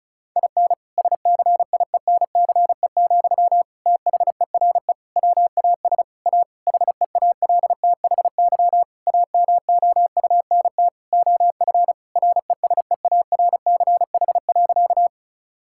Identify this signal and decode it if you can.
Morse code, heard as a keyed tone, about 36 wpm, 700 Hz. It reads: IN SCIENCE, THERE WAS A HEALTHY AMOUNT OF RESEARCH.
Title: Sentences from Top 1000 Words - Review Part 5 - 35wpm